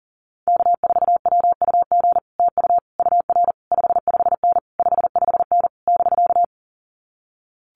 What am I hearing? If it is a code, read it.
Morse code, heard as a keyed tone, about 40 wpm, 700 Hz. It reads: K4WUG TU UR 55N 55N <BK>
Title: Common QSO Phrases Encore 40wpm